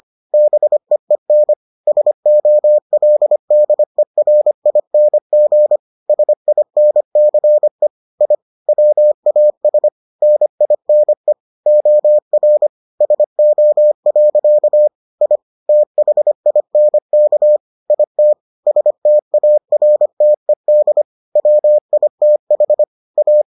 Morse code, heard as a keyed tone, about 25 wpm, 600 Hz. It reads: BEEN SOLDERING SINCE I WAS NINE OR SO. I THINK IT STARTED WITH A